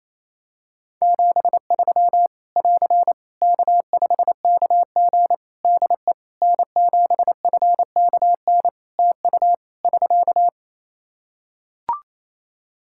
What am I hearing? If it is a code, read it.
Morse code, heard as a keyed tone, about 28 wpm, 700 Hz. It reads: 73 <AR> K5KG DE N7FKN TU <SK>
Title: K1USN SST QSO Practice 28wpm